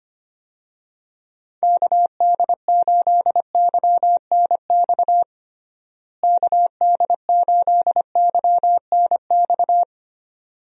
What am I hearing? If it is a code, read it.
Morse code, heard as a keyed tone, about 25 wpm, 700 Hz. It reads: KD8YNX KD8YNX
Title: USA Call Signs - 6 Characters - Repeated 2x - 25wpm